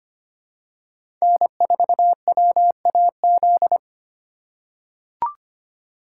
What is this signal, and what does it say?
Morse code, heard as a keyed tone, about 25 wpm, 700 Hz. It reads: N4WAZ
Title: USA Call Signs - 8x Character Spacing - 25wpm